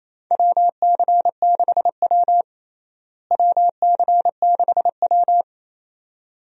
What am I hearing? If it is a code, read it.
Morse code, heard as a keyed tone, about 28 wpm, 700 Hz. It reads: WC6W WC6W
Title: USA Call Signs - 4 Characters - Repeated 2x - 28wpm